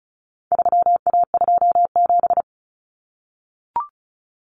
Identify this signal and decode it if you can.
Morse code, heard as a keyed tone, about 34 wpm, 700 Hz. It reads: 3A27
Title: Common Cut Numbers - 4-Digit 35wpm